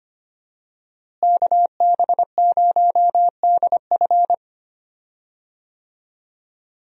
Morse code, heard as a keyed tone, about 25 wpm, 700 Hz. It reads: KB0DF
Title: USA Call Signs - 5 Characters - Repeated 2x - 25wpm